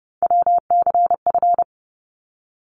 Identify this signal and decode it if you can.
Morse code, heard as a keyed tone, about 30 wpm, 700 Hz. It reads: WCF
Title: ARRL Sections 30wpm